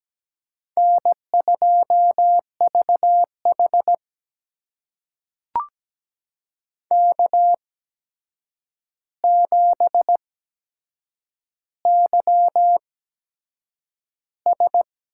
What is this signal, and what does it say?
Morse code, heard as a keyed tone, about 17 wpm, spaced as for 7 wpm, 700 Hz. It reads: N2VH K7YS
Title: USA Call Signs - 4 Characters - 8x Character Spacing - 17wpm